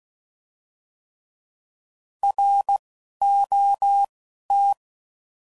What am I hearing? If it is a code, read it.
Morse code, heard as a keyed tone, about 16 wpm, 800 Hz. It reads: R O T
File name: Prefixes / By First Letter / Rs